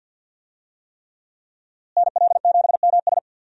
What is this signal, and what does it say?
Morse code, heard as a keyed tone, about 47 wpm, 700 Hz. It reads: NL7MS
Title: USA Call Signs - 5 Characters - Repeated 3x - 50wpm